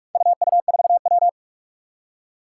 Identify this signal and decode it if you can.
Morse code, heard as a keyed tone, about 44 wpm, 700 Hz. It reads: UU4W
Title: Four Challenging Letter-Number Combinations 45wpm